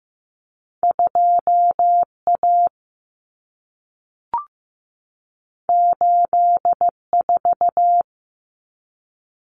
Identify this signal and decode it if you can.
Morse code, heard as a keyed tone, about 15 wpm, 700 Hz. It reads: 2A 84
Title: Common Cut Numbers - 2-Digit 15wpm